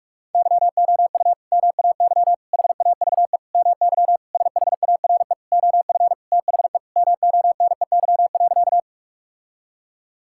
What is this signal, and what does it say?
Morse code, heard as a keyed tone, about 46 wpm, 700 Hz. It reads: YOU MAY HAVE MY SHARE OF THE MONEY.
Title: Sentences from Top 1000 Words - Review Part 6 - 45wpm